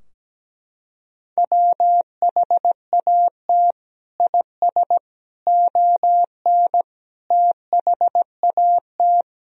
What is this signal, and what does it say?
Morse code, heard as a keyed tone, about 17 wpm, 700 Hz. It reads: WHAT IS ON THAT
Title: Sentences from Top 1000 Words - Review Part 4 - 17wpm